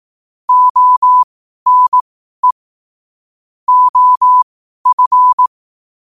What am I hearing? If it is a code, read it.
Morse code, heard as a keyed tone, about 18 wpm, spaced as for 12 wpm, 1000 Hz. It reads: ONE OF
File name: qotdfiles / 12wpm